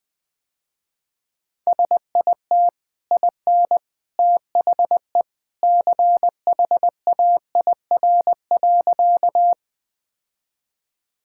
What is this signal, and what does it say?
Morse code, heard as a keyed tone, about 21 wpm, 700 Hz. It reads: SIT IN THE CHAIR.
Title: Easy Sentences with Hints - 20wpm